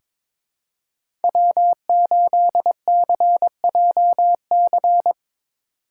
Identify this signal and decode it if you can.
Morse code, heard as a keyed tone, about 22 wpm, 700 Hz. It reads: W8CJC
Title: USA Call Signs - 8x Character Spacing - 22wpm